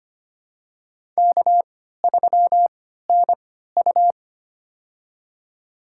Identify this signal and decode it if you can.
Morse code, heard as a keyed tone, about 25 wpm, spaced as for 14 wpm, 700 Hz. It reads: K3NU
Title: USA Call Signs - 4 Characters - 3x Character Spacing - 25wpm